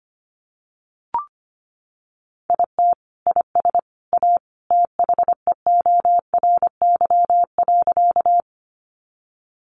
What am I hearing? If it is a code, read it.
Morse code, heard as a keyed tone, about 25 wpm, 700 Hz. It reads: IT IS A THEORY.
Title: Sentences from Top 600 Words 25wpm